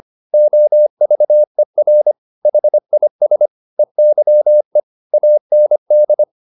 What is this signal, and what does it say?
Morse code, heard as a keyed tone, about 25 wpm, 600 Hz. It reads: OVER HIS EYE AND